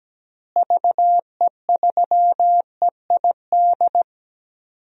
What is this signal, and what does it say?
Morse code, heard as a keyed tone, about 17 wpm, 700 Hz. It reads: VE3EID
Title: DXCC-100 Call Signs - 6 Characters - Repeated 3x - 17wpm